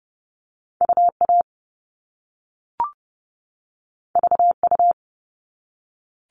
Morse code, heard as a keyed tone, about 30 wpm, 700 Hz. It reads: UA VU
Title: All Cut Numbers - 2-Digit 30wpm